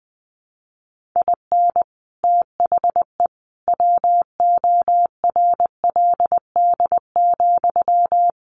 Morse code, heard as a keyed tone, about 20 wpm, 700 Hz. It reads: IN THE WORLD,
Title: Sentences from Top 1400 Words 20wpm